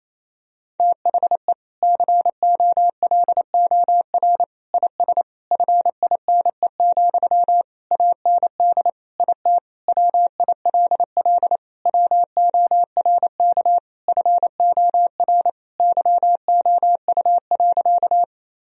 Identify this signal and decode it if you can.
Morse code, heard as a keyed tone, about 28 wpm, 700 Hz. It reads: THE COLOR IS FINE, AND IT WILL WORK FOR YOU.
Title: Sentences from Top 1000 Words - Review Part 2 - 28wpm